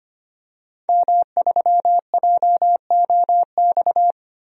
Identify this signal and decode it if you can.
Morse code, heard as a keyed tone, about 25 wpm, 700 Hz. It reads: M3JOX
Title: DXCC-100 Call Signs - 5 Characters - 25wpm